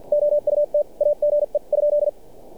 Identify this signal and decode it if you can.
Morse code, heard as a keyed tone, about 50 wpm, spaced as for 41 wpm, 590 Hz. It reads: OUTAGE ?
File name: archived / CRSnet